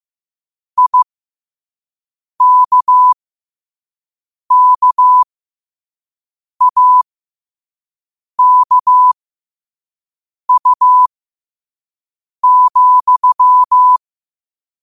Morse code, heard as a keyed tone, about 15 wpm, spaced as for 5 wpm, 1000 Hz. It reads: IKKAKU,
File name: qotdfiles / 5wpm